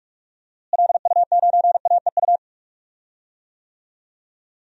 Morse code, heard as a keyed tone, about 46 wpm, 700 Hz. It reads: LU9AEU
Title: DXCC-100 Call Signs - 6 Characters - Repeated 2x - 45wpm